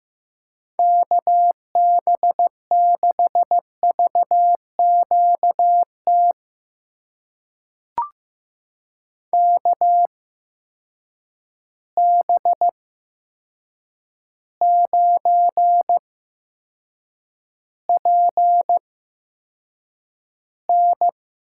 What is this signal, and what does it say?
Morse code, heard as a keyed tone, about 15 wpm, spaced as for 7 wpm, 700 Hz. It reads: KB6VQT KB9PN
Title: USA Call Signs - 6 Characters - 8x Character Spacing - 15wpm